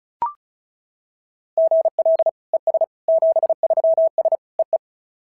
Morse code, heard as a keyed tone, about 36 wpm, 690 Hz. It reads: GL ES 73S EE
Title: SOTA QSO Practice 35wpm